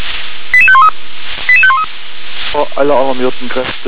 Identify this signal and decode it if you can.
Five-tone selective call (ZVEI), heard as quick runs of five tones, tones of about 70 ms, with speech. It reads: 88513 88513